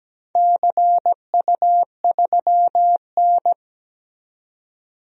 Morse code, heard as a keyed tone, about 17 wpm, 700 Hz. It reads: CU3N
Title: DXCC-100 Call Signs - 4 Characters - 17wpm